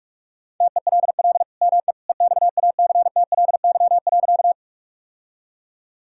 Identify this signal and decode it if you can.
Morse code, heard as a keyed tone, about 41 wpm, 700 Hz. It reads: TELL ME EXACTLY.
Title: Sentences from Top 1000 Words - Review Part 6 - 45wpm